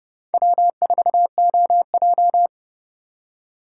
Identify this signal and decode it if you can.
Morse code, heard as a keyed tone, about 30 wpm, 700 Hz. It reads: W4OJ